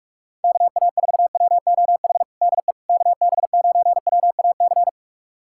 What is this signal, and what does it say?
Morse code, heard as a keyed tone, about 43 wpm, 700 Hz. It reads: KA4WOH DE KB9WAC
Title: USA Callsign DE Callsign 45wpm